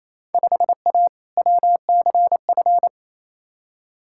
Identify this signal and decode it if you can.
Morse code, heard as a keyed tone, about 28 wpm, 700 Hz. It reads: 5A WCF
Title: Field Day Exchanges Encore - 28wpm